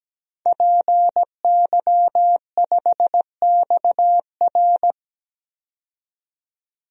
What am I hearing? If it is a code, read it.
Morse code, heard as a keyed tone, about 17 wpm, 700 Hz. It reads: PY5XR
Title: DXCC-100 Call Signs - 5 Characters - 17wpm